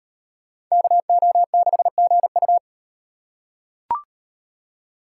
Morse code, heard as a keyed tone, about 36 wpm, 700 Hz. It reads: KO6GU